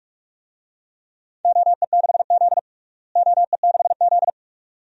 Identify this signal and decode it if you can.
Morse code, heard as a keyed tone, about 46 wpm, 700 Hz. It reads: OE6Z OE6Z
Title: DXCC-100 Call Signs - 4 Characters - Repeated 2x - 45wpm